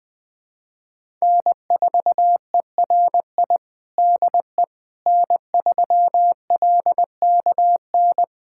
Morse code, heard as a keyed tone, about 20 wpm, 700 Hz. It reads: N4ERI DE N3LKN
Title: USA Callsign DE Callsign - 5 Characters - 20wpm